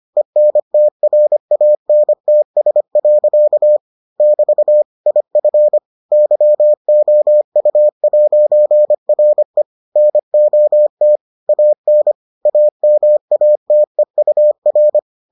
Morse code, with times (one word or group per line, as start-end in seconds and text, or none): 0.2-3.8 ENTRANTS.
4.2-4.8 <BT>
5.1-5.8 IF
6.1-9.6 YOU'RE
10.0-11.2 NOT
11.5-12.1 AN
12.5-15.0 AMATEUR